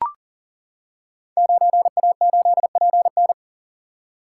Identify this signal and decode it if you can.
Morse code, heard as a keyed tone, about 42 wpm, 700 Hz. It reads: 9A8PN
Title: Code Groups - 5 Letter:Numbers - 40wpm